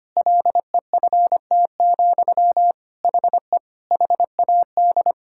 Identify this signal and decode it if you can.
Morse code, heard as a keyed tone, about 25 wpm, 700 Hz. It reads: LEFT, HE HAD
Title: Sentences from Top 1000 Words - Review Part 5 - 25wpm